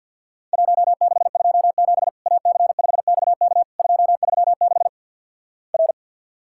Morse code, heard as a keyed tone, about 50 wpm, 700 Hz. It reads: 1627 AC5XK 236 R
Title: Worked All Europe DX Contest - QTC Exchange - 50wpm